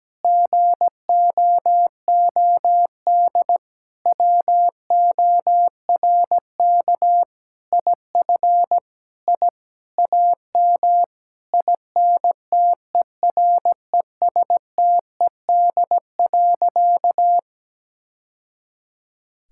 Morse code, heard as a keyed tone, about 17 wpm, 700 Hz. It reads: GOOD WORK IF I AM INTERESTED.
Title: Sentences from Top 1000 Words - Review Part 6 - 17wpm